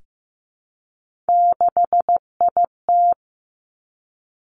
Morse code, heard as a keyed tone, about 15 wpm, 700 Hz. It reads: 6IT
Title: Three Challenging Letter-Number Combinations 15wpm